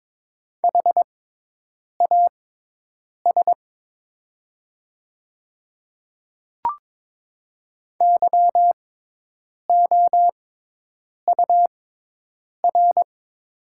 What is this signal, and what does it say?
Morse code, heard as a keyed tone, about 22 wpm, spaced as for 8 wpm, 700 Hz. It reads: HAS YOUR
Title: 6x Character Spacing - Top 100 Words - 22wpm